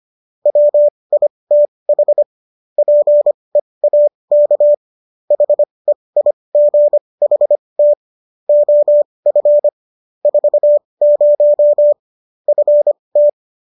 Morse code, heard as a keyed tone, about 25 wpm, spaced as for 20 wpm, 590 Hz. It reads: WITH PEAK HEIGHT OF 40 FT
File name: qso221 20